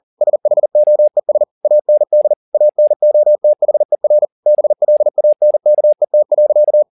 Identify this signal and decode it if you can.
Morse code, heard as a keyed tone, about 40 wpm, 600 Hz. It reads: SHOES AND ANOTHER BLANKET.